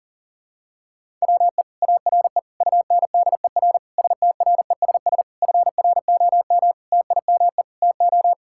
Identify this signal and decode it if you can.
Morse code, heard as a keyed tone, about 40 wpm, 700 Hz. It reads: WE ARE UNDER STRESS FROM TIME TO
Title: Sentences from Top 1000 Words - Review Part 6 - 40wpm